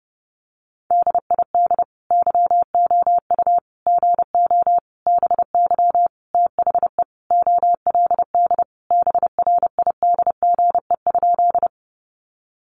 Morse code, heard as a keyed tone, about 30 wpm, 700 Hz. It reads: DID YOU GO BY THE OLD BRIDGE?
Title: Sentences from Top 1200 Words 30wpm